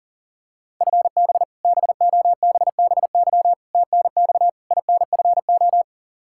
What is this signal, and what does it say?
Morse code, heard as a keyed tone, about 40 wpm, 700 Hz. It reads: FB BOBBY TNX INFO
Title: Practice QSO - Exchange 3 - More Info - 40wpm